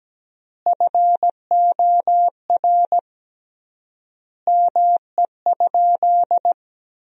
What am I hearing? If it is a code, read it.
Morse code, heard as a keyed tone, about 17 wpm, 700 Hz. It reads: FOR ME?
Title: 3x Word-Spacing - Sentences from Top 500 Words - 17wpm